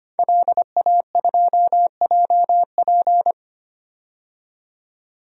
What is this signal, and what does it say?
Morse code, heard as a keyed tone, about 25 wpm, 700 Hz. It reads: LA2JP
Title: DXCC-100 Call Signs - 5 Characters - 25wpm